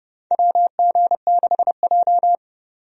Morse code, heard as a keyed tone, about 30 wpm, 700 Hz. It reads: WG6J
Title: USA Call Signs - 4 Characters - 30wpm